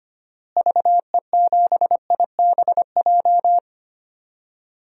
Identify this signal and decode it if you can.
Morse code, heard as a keyed tone, about 25 wpm, 700 Hz. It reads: VE7IBJ
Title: DXCC-100 Call Signs - 6 Characters - Repeated 3x - 25wpm